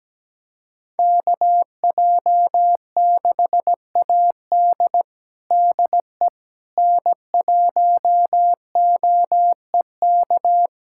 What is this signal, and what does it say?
Morse code, heard as a keyed tone, about 17 wpm, 700 Hz. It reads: KJ6AD DE N1OEK
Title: USA Callsign DE Callsign - 5 Characters - 17wpm